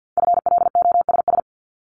Morse code, heard as a keyed tone, about 50 wpm, 700 Hz. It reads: FLOSS
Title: Multi-letter Phonemes 50wpm